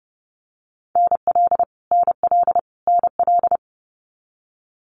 Morse code, heard as a keyed tone, about 30 wpm, 700 Hz. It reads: NL NL NL
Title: Canadian Provinces and Territories - Repeated 3x - 30wpm